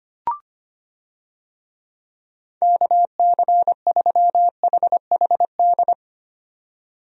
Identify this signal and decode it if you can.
Morse code, heard as a keyed tone, about 25 wpm, 700 Hz. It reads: KC3HHD
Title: USA Call Signs - 6 Characters - 25wpm